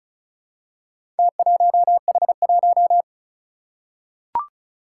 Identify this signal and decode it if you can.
Morse code, heard as a keyed tone, about 36 wpm, 700 Hz. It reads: T1H1
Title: Four Challenging Letter-Number Combinations 35wpm